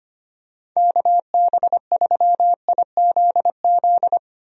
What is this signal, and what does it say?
Morse code, heard as a keyed tone, about 25 wpm, 700 Hz. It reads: KB3IZZ
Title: USA Call Signs - 6 Characters - Repeated 2x - 25wpm